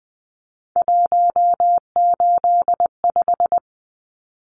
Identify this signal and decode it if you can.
Morse code, heard as a keyed tone, about 20 wpm, 700 Hz. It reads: 185
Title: Numbers - 3-Digit 20wpm